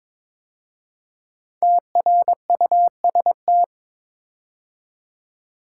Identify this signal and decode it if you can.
Morse code, heard as a keyed tone, about 22 wpm, 700 Hz. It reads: TRUST
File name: 5 Letter Words - First 3 Letters Given - 22wpm